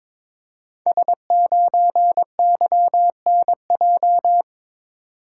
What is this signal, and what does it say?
Morse code, heard as a keyed tone, about 22 wpm, 700 Hz. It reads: S9YNJ
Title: Code Groups - 5 Letter:Numbers - 22wpm